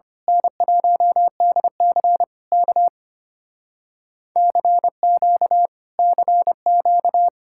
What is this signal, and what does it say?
Morse code, heard as a keyed tone, about 24 wpm, 700 Hz. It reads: N1DC K CQ CQ
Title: Speed-Racing Common QSO Phrases Encore - 25wpm